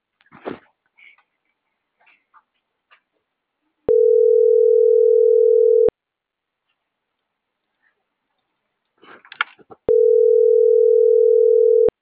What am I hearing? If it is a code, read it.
Ringback tone (North America), 440 + 480 Hz.